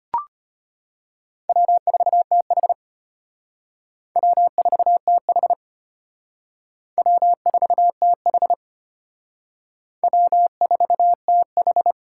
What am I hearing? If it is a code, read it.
Morse code, heard as a keyed tone, about 31 wpm, 700 Hz. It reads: W4TH W4TH W4TH W4TH